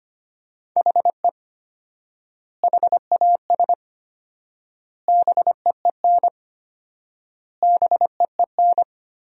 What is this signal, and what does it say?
Morse code, heard as a keyed tone, about 25 wpm, 700 Hz. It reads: HE HAS BEEN BEEN
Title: 4x Word-Spacing - Sentences from Top 300 Words - 25wpm